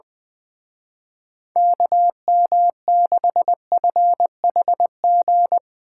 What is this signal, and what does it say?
Morse code, heard as a keyed tone, about 21 wpm, 700 Hz. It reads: KM6FHG
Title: Speed Racing - USA 2x3 Call Signs 20wpm